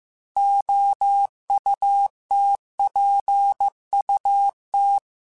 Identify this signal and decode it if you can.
Morse code, heard as a keyed tone, about 15 wpm, 800 Hz. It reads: OUTPUT
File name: Technical Subjects / Technical Subjects-mp3s / Gates Intro @13